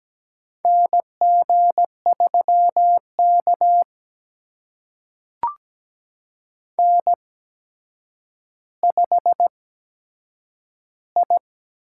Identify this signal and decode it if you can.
Morse code, heard as a keyed tone, about 17 wpm, spaced as for 8 wpm, 700 Hz. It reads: NG3K N5I
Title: USA Call Signs - 4 Characters - 8x Character Spacing - 17wpm